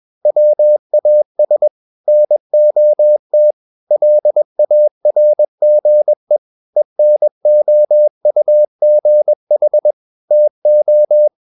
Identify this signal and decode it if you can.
Morse code, heard as a keyed tone, about 21 wpm, 600 Hz. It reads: WAS NOT LARGE ENOUGH TO
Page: CW med Gnister